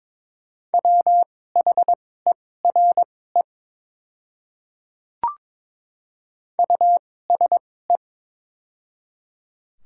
Morse code, heard as a keyed tone, about 22 wpm, spaced as for 16 wpm, 700 Hz. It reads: WHERE USE
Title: 2x Character Spacing - Top 200 Words - 22wpm